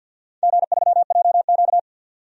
Morse code, heard as a keyed tone, about 51 wpm, 700 Hz. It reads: G3JQ
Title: DXCC-100 Call Signs - 4 Characters - 50wpm